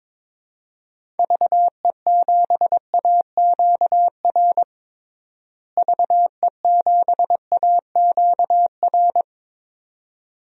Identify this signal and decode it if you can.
Morse code, heard as a keyed tone, about 22 wpm, 700 Hz. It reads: VE7AQR VE7AQR
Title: DXCC-100 Call Signs - 6 Characters - Repeated 2x - 22wpm